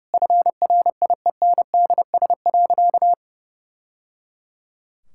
Morse code, heard as a keyed tone, about 30 wpm, 700 Hz. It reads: FRIENDS.
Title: Sentences from Top 1000 Words - Review Part 5 - 30wpm